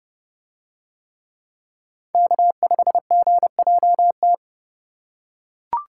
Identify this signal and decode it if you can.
Morse code, heard as a keyed tone, about 27 wpm, 700 Hz. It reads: K5GJT